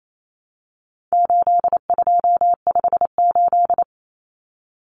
Morse code, heard as a keyed tone, about 28 wpm, 700 Hz. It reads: 8258
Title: Common Cut Numbers - 4-Digit 28wpm